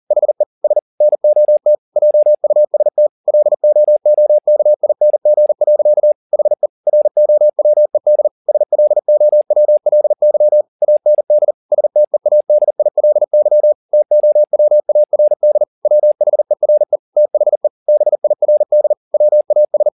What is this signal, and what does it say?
Morse code, heard as a keyed tone, about 40 wpm, 600 Hz. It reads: HE S NOT JUST LOOKING. HE ROWED SLOWLY AND STEADILY TOWARD WHERE THE BIRD WAS